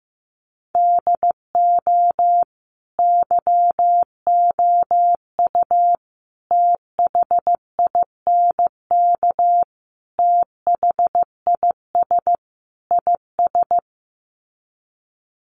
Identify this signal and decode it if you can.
Morse code, heard as a keyed tone, about 15 wpm, 700 Hz. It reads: DO YOU THINK THIS IS
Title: Sets of 5 Words Encore2 15wpm